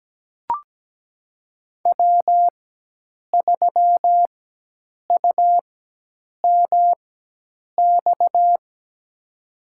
Morse code, heard as a keyed tone, about 17 wpm, spaced as for 8 wpm, 700 Hz. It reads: W3UMX
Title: USA Call Signs - 5 Characters - 4x Character Spacing - 17wpm